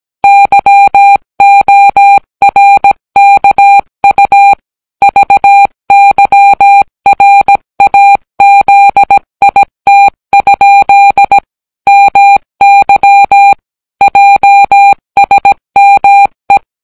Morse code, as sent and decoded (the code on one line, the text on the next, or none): -.-- --- .-. -.- ..- / ...- -.-- .-. .- --.. .. - ..--.. / -- -.-- / .--- ... -- .
YORKU VYRAZIT? MY JSME